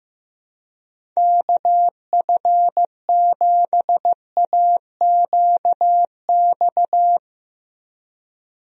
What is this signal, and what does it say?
Morse code, heard as a keyed tone, about 15 wpm, 700 Hz. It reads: KF7AQX
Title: USA Call Signs - 6 Characters - Repeated 3x - 15wpm